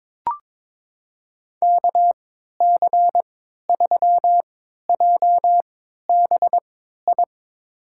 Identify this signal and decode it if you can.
Morse code, heard as a keyed tone, about 22 wpm, spaced as for 13 wpm, 700 Hz. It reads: KC3JBI
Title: USA Call Signs - 6 Characters - 3x Character Spacing - 22wpm